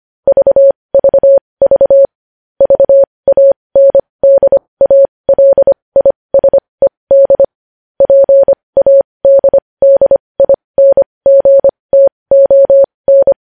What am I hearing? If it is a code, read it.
Morse code, heard as a keyed tone, about 25 wpm, spaced as for 20 wpm, 550 Hz. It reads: VVV VANDALISED PADDINGTON
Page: news headlines in morse code at 20 wpm